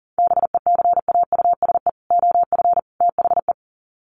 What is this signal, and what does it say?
Morse code, heard as a keyed tone, about 40 wpm, 700 Hz. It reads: BECAUSE OF THE
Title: Sets of 3 Words 40wpm